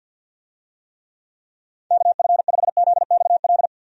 Morse code, heard as a keyed tone, about 51 wpm, 700 Hz. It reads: KF5ZXL
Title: USA Call Signs - 6 Characters - 50wpm